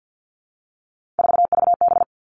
Morse code, heard as a keyed tone, about 50 wpm, 700 Hz. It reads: 4VB
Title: Three Challenging Letter-Number Combinations 50wpm